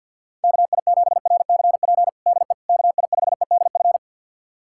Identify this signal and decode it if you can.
Morse code, heard as a keyed tone, about 52 wpm, 700 Hz. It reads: KI7RCP DE KI5EDF
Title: USA Callsign DE Callsign 50wpm